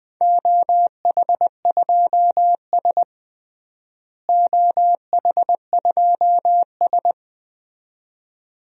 Morse code, heard as a keyed tone, about 20 wpm, 700 Hz. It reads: OH2S OH2S
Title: DXCC-100 Call Signs - 4 Characters - Repeated 2x - 20wpm